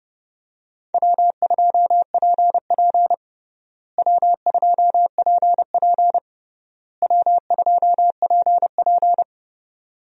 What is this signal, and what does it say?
Morse code, heard as a keyed tone, about 31 wpm, 700 Hz. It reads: W2PP W2PP W2PP
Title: USA Call Signs - 4 Characters - Repeated 3x - 30wpm